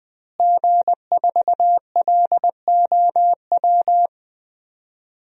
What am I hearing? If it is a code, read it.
Morse code, heard as a keyed tone, about 20 wpm, 700 Hz. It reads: G4LOW
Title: DXCC-100 Call Signs - 5 Characters - Repeated 2x - 20wpm